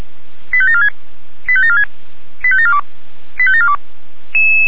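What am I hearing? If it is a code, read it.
Five-tone selective call (ZVEI), heard as quick runs of five tones, tones of about 70 ms. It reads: 85746 85746 85742 85742
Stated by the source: ZVEI five-tone selective call received on an SDR.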